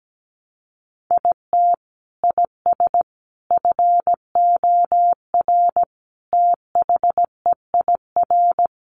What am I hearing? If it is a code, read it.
Morse code, heard as a keyed tone, about 17 wpm, 700 Hz. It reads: IT IS FOR THEIR
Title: Sentences from Top 700 Words 17wpm